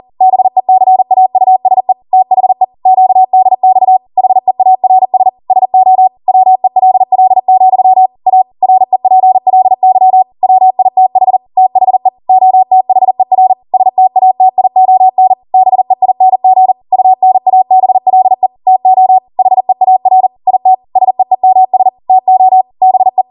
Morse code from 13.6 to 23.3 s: STATION BEING UNABLE TO HEAR IT SEEMS TO BE